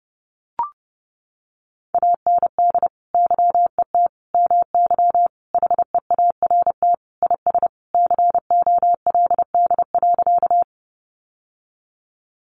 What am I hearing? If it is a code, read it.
Morse code, heard as a keyed tone, about 30 wpm, 700 Hz. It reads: AND YET MY HEART IS COLD.
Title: Sentences from Top 1200 Words 30wpm